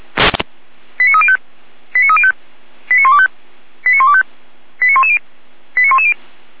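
Five-tone selective call (ZVEI), heard as quick runs of five tones, tones of about 70 ms. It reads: 89305 89305 89125 89125 89110 89110
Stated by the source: ZVEI five-tone selective call received on SDR.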